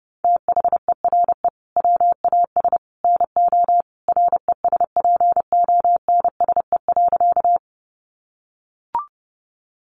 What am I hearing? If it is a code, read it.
Morse code, heard as a keyed tone, about 30 wpm, 700 Hz. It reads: THERE WAS NO RESPONSE.
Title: Sentences from Top 1400 Words 30wpm